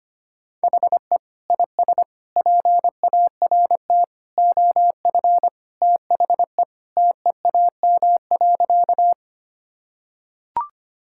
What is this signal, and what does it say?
Morse code, heard as a keyed tone, about 25 wpm, 700 Hz. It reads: HE IS PART OF THE TEAM.
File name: practice / 25wpm / Sentences with 3-Word Phrases - 25wpm